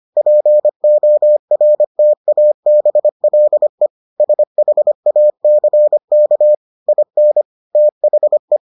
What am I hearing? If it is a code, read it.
Morse code, heard as a keyed tone, about 25 wpm, 600 Hz. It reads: PORTABLE SHACK IN THE